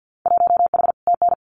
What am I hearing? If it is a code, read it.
Morse code, heard as a keyed tone, about 50 wpm, 700 Hz. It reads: 1H TN
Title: Winter Field Day Exchanges 50wpm